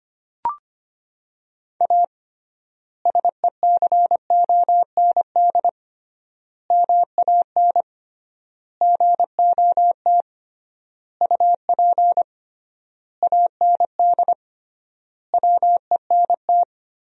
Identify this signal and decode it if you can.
Morse code, heard as a keyed tone, about 25 wpm, 700 Hz. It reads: A SECOND MAN GOT UP AND WENT
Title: 3x Word-Spacing - Sentences from Top 300 Words - 25wpm